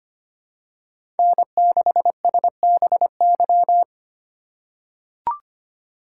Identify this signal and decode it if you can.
Morse code, heard as a keyed tone, about 25 wpm, 700 Hz. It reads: N6SBY
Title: USA Call Signs - 5 Characters - 2x Character Spacing - 25wpm